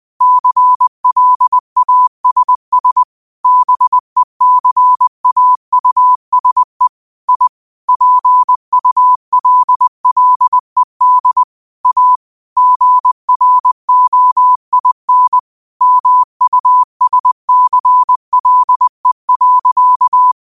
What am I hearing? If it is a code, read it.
Morse code, heard as a keyed tone, about 20 wpm, 1000 Hz. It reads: CLASS BECAUSE I PULLED A GROIN MUSCLE.
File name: qotdfiles / 20wpm